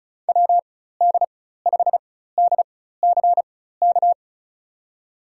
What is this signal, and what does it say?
Morse code, heard as a keyed tone, about 34 wpm, spaced as for 16 wpm, 700 Hz. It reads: WD5DCK
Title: USA Call Signs - 4x Character Spacing - 35wpm